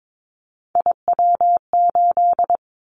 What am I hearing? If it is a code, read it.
Morse code, heard as a keyed tone, about 22 wpm, 700 Hz. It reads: IW8
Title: Callsign Prefixes for Top 25 DXCC - 22wpm